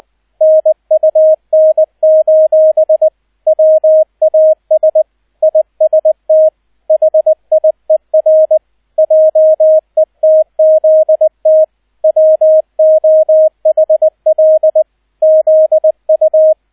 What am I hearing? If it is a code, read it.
Morse code, heard as a keyed tone, about 20 wpm, 630 Hz, with radio noise. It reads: NUN: WAS IST HIER JETZT WOHL ZU